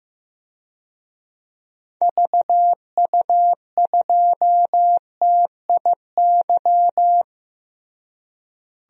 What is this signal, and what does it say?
Morse code, heard as a keyed tone, about 15 wpm, 700 Hz. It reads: VU2TIY
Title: DXCC-100 Call Signs - 6 Characters - Repeated 3x - 15wpm